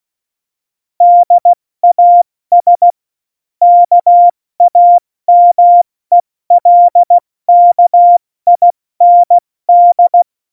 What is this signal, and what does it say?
Morse code, heard as a keyed tone, about 16 wpm, 700 Hz. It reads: DAS KAMELKIND
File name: Morsekurs / Witze / Witz 06. November 2018, 70 F80